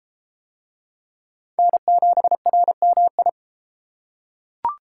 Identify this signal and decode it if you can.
Morse code, heard as a keyed tone, about 32 wpm, 700 Hz. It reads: N7RMI